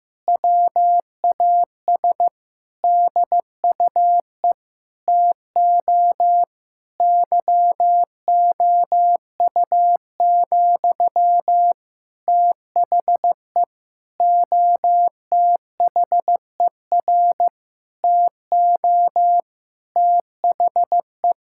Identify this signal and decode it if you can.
Morse code, heard as a keyed tone, about 15 wpm, 700 Hz. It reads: WAS DUE TO YOU, THE OTHER TO THE
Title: Sentences from Top 1000 Words - Review Part 6 - 15wpm